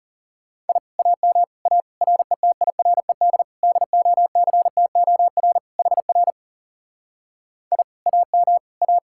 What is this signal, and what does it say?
Morse code, heard as a keyed tone, about 38 wpm, 700 Hz. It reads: I AM A RETIRED DOCTOR HR I AM A